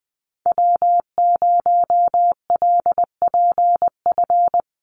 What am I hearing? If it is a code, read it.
Morse code, heard as a keyed tone, about 20 wpm, 700 Hz. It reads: W0LPF
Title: Call Signs Repeated 3x - 20wpm